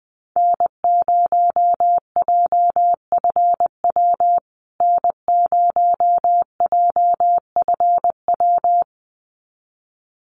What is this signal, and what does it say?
Morse code, heard as a keyed tone, about 20 wpm, 700 Hz. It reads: N0JFW N0JFW
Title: USA Call Signs Repeated 2x 20wpm